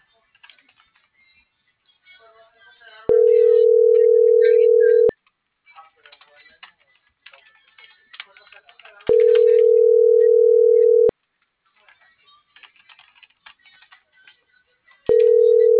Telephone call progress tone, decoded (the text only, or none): ringback tone